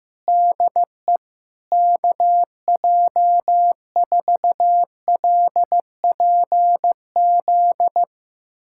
Morse code, heard as a keyed tone, about 15 wpm, 700 Hz. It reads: DE KJ4LPZ
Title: USA Callsign DE Callsign - 6 Characters - 15wpm